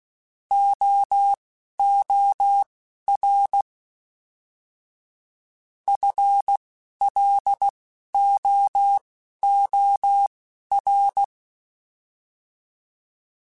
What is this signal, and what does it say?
Morse code, heard as a keyed tone, about 16 wpm, spaced as for 12 wpm, 800 Hz. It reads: OOR FLOOR
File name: FL / FL mp3s / Fast FL @10